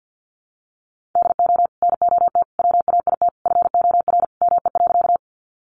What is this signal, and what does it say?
Morse code, heard as a keyed tone, about 48 wpm, 700 Hz. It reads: DO NOT WAIT FOR ME.
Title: Sentences from Top 600 Words 50wpm